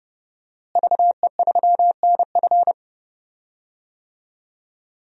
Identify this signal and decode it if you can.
Morse code, heard as a keyed tone, about 31 wpm, 700 Hz. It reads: VE3NF
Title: DXCC-100 Call Signs - 5 Characters - Repeated 2x - 30wpm